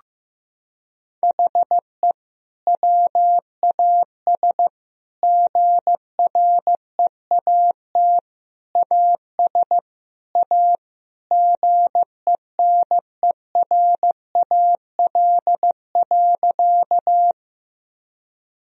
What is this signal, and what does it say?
Morse code, heard as a keyed tone, about 15 wpm, 700 Hz. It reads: HE WAS GREAT AS A GENERAL.
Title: Sentences from Top 1000 Words - Review Part 3 - 15wpm